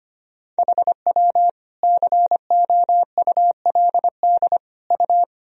Morse code, heard as a keyed tone, about 25 wpm, 700 Hz. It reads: HW COULD U
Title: Sentences from Top 300 Words and Abbreviations 25wpm